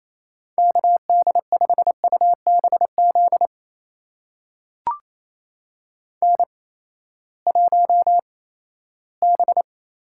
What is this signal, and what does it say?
Morse code, heard as a keyed tone, about 28 wpm, spaced as for 16 wpm, 700 Hz. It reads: KD5UBZ N1B
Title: USA Call Signs - 8x Character Spacing - 28wpm